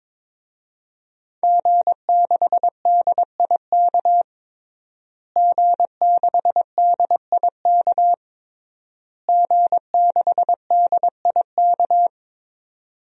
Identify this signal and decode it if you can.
Morse code, heard as a keyed tone, about 22 wpm, 700 Hz. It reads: G6DIK G6DIK G6DIK
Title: DXCC-100 Call Signs - 5 Characters - Repeated 3x - 22wpm